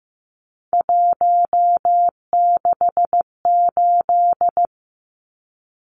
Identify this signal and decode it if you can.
Morse code, heard as a keyed tone, about 15 wpm, 700 Hz. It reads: 168
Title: Numbers - 3-Digit 15wpm